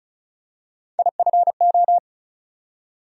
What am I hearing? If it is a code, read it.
Morse code, heard as a keyed tone, about 34 wpm, 700 Hz. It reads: IFO
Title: USA Call Sign Postfixes - 3 Letters 35wpm